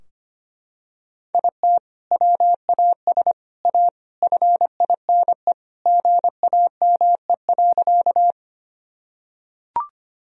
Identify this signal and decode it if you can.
Morse code, heard as a keyed tone, about 25 wpm, 700 Hz. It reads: IT WAS A FINE GAME.
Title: Sentences from Top 1000 Words - Review Part 3 - 25wpm